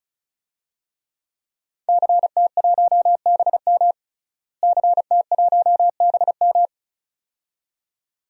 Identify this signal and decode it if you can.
Morse code, heard as a keyed tone, about 36 wpm, 700 Hz. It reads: CT1BM CT1BM
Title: DXCC-100 Call Signs - 5 Characters - Repeated 2x - 35wpm